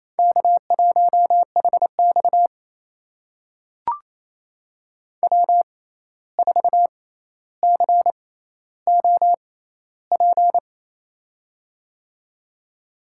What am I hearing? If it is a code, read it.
Morse code, heard as a keyed tone, about 28 wpm, spaced as for 13 wpm, 700 Hz. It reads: K1HX W4COP
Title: USA Call Signs - 6x Character Spacing - 28wpm